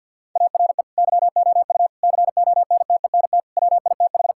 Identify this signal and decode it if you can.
Morse code, heard as a keyed tone, about 50 wpm, 700 Hz. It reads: ARE YOU CONTENT WITH
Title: Sentences from Top 1000 Words - Review Part 6 - 50wpm